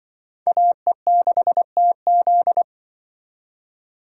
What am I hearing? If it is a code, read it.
Morse code, heard as a keyed tone, about 24 wpm, 700 Hz. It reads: AE6TZ
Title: Speed Racing - USA 2x2 Call Signs 20wpm